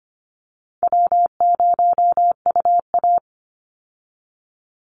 Morse code, heard as a keyed tone, about 25 wpm, 700 Hz. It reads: W0UA
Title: Call Signs - 25wpm